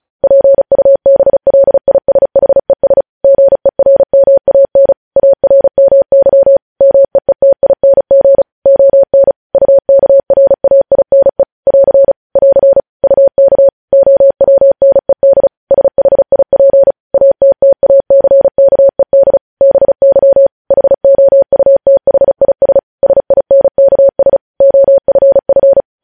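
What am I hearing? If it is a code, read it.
Morse code, heard as a keyed tone, about 36 wpm, 550 Hz. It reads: PUBLISHES GERMAN ARMY MEETING ON UKRAINE <AR> <AR> UK OWNED SHIP ATTACKED BY HOUTHIS SINKS OFF
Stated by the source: Morse practice at 35 wpm sending news headlines.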